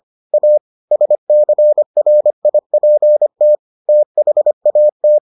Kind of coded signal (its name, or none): Morse code